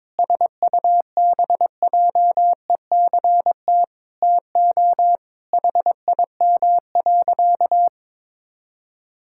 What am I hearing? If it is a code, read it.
Morse code, heard as a keyed tone, about 22 wpm, 700 Hz. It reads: SUBJECT TO HIM.
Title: Sentences from Top 1600 Words 22wpm